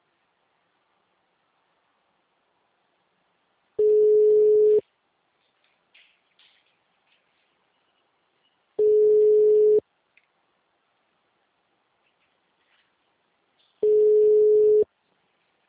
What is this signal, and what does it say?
Ringback tone (Europe, 425 Hz).
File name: out- -2034- - - .50230